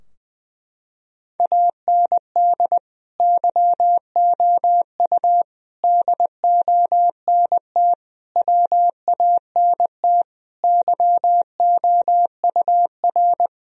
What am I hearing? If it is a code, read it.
Morse code, heard as a keyed tone, about 20 wpm, 700 Hz. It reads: AND YOU DONT WANT YOUR
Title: Sentences from Top 1000 Words - Review Part 2 - 20wpm